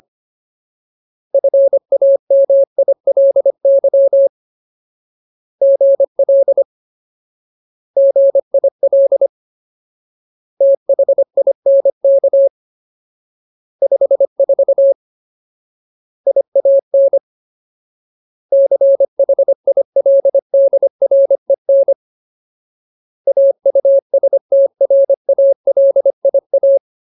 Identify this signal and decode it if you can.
Morse code, heard as a keyed tone, about 25 wpm, 550 Hz. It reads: FAMILY GL GIL THINK 54 IAN CHILDREN AUSTRALIA
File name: Ultimate Word List / 25WPM Ultimate